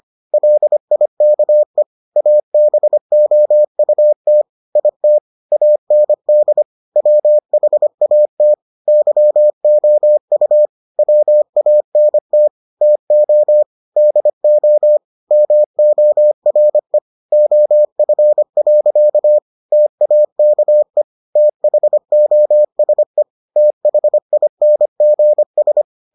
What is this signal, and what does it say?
Morse code, heard as a keyed tone, about 25 wpm, 600 Hz. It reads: LIKE ABOUT IT AND WHAT YOU WANT TO DO MORE OF. TAKE THOSE THINGS